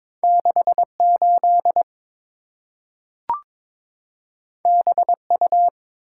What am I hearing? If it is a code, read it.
Morse code, heard as a keyed tone, about 23 wpm, 690 Hz. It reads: 68 BU
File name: practice / 22wpm / Two Letter-Number Combinations - 22wpm